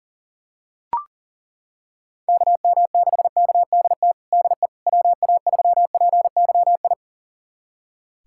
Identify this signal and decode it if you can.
Morse code, heard as a keyed tone, about 40 wpm, 700 Hz. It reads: KM6KDT DE WA3PYI
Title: USA Callsign DE Callsign - 6 Characters - 40wpm